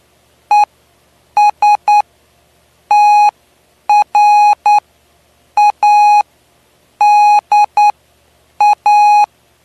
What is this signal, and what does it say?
Morse code, heard as a keyed tone, about 9 wpm, spaced as for 7 wpm, 830 Hz. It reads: ESTRADA
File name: cw / py2lcd / 09-PY2LCD-Li��o 9